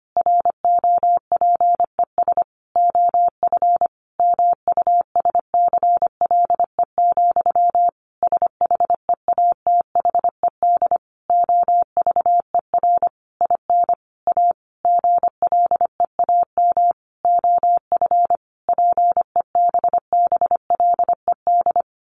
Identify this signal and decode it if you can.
Morse code, heard as a keyed tone, about 25 wpm, 700 Hz. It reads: ROPES OF MUSCLE, SHEATHED OVER IN A GLEAM OF PEBBLED
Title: A Sound Of Thunder 25wpm